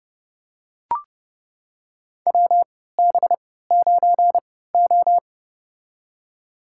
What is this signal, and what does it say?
Morse code, heard as a keyed tone, about 29 wpm, spaced as for 17 wpm, 700 Hz. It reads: WB9O
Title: USA Call Signs - 4 Characters - 3x Character Spacing - 30wpm